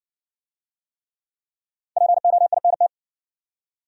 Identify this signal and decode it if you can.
Morse code, heard as a keyed tone, about 60 wpm, 700 Hz. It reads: POINT